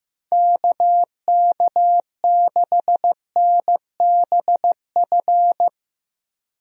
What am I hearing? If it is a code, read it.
Morse code, heard as a keyed tone, about 15 wpm, 700 Hz. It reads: KK6NBF
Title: USA Call Signs - 6 Characters - Repeated 2x - 15wpm